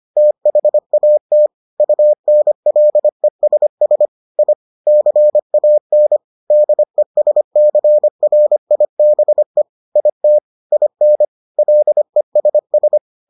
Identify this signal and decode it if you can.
Morse code, heard as a keyed tone, about 25 wpm, 600 Hz. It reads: THAT UNLESS I CAN DESCRIBE IT IN LESS